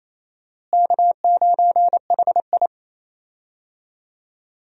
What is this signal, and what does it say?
Morse code, heard as a keyed tone, about 28 wpm, 700 Hz. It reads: K9HI
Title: USA Call Signs - 4 Characters - Repeated 2x - 28wpm